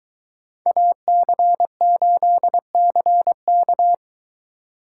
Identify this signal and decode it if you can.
Morse code, heard as a keyed tone, about 22 wpm, 700 Hz. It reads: AC8CK